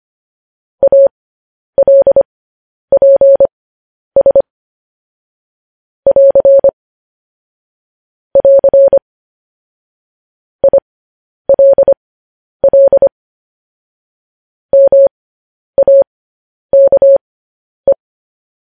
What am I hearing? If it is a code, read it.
Morse code, heard as a keyed tone, about 25 wpm, spaced as for 10 wpm, 550 Hz. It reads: ALPS <AR> <AR> ILL MAKE
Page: news-headlines-in-morse-code-at-10-wpm